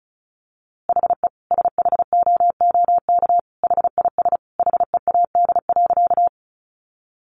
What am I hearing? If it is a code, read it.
Morse code, heard as a keyed tone, about 34 wpm, 700 Hz. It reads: HE SHOOK HIS HEAD.
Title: A Sound Of Thunder 35wpm